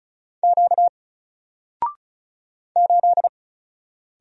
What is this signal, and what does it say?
Morse code, heard as a keyed tone, about 36 wpm, 700 Hz. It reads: Q 8
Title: Single Letter-Number Alt 35wpm